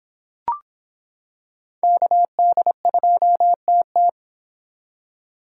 Morse code, heard as a keyed tone, about 26 wpm, 700 Hz. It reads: KD2TT